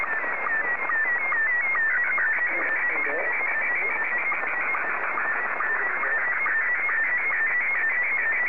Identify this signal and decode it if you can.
SSTV (slow-scan television), Scottie 1, with speech.